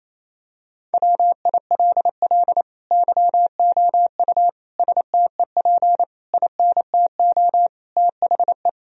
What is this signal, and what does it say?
Morse code, heard as a keyed tone, about 28 wpm, 700 Hz. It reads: WILL YOU STEP INTO THE
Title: Sentences from Top 1000 Words - Review Part 2 - 28wpm